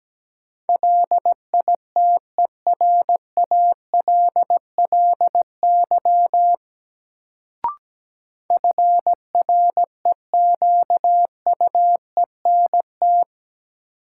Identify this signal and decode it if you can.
Morse code, heard as a keyed tone, about 17 wpm, 700 Hz. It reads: LITERALLY FREQUENT
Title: Sentences from Top 1400 Words - New Word Drill - 17wpm